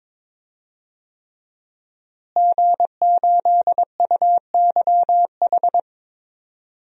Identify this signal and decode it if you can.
Morse code, heard as a keyed tone, about 22 wpm, 700 Hz. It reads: G8UYH
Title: DXCC-100 Call Signs - 5 Characters - Repeated 3x - 22wpm